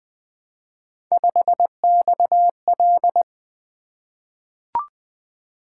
Morse code, heard as a keyed tone, about 20 wpm, 700 Hz. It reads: 5XL
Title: Three Letter-Number Combinations - 20wpm